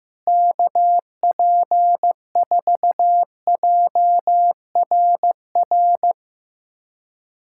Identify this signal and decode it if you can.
Morse code, heard as a keyed tone, about 15 wpm, 700 Hz. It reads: KP4JRR
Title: USA Callsign DE Callsign - 6 Characters - 15wpm